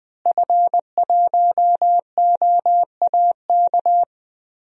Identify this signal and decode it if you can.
Morse code, heard as a keyed tone, about 20 wpm, 700 Hz. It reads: F1OAK
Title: DXCC-100 Call Signs - 5 Characters - Repeated 3x - 20wpm